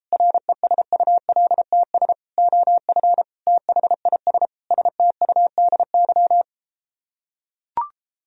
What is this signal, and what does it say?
Morse code, heard as a keyed tone, about 33 wpm, 700 Hz. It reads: RESULTS OF THIS STUDY